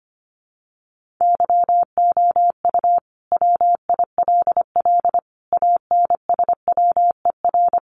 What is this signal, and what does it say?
Morse code, heard as a keyed tone, about 25 wpm, 700 Hz. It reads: YOU WILL ANSWER
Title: Sentences from Top 300 Words 25wpm